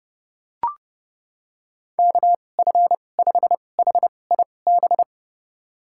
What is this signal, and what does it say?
Morse code, heard as a keyed tone, about 30 wpm, spaced as for 22 wpm, 700 Hz. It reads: KF5HIB
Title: USA Call Signs - 2x Character Spacing - 30wpm